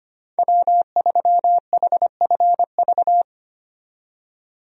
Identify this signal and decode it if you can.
Morse code, heard as a keyed tone, about 25 wpm, 690 Hz. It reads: W3HFV
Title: USA Call Signs - 5 Characters - 25wpm